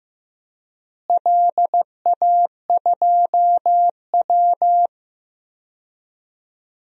Morse code, heard as a keyed tone, about 15 wpm, 700 Hz. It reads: LA2W
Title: DXCC-100 Call Signs - 4 Characters - Repeated 3x - 15wpm